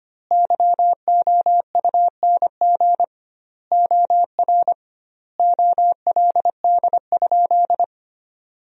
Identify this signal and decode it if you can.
Morse code, heard as a keyed tone, about 25 wpm, 700 Hz. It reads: YOUNG OR OLD?
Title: 2x Word-Spacing - Sentences from Top 400 Words - 25wpm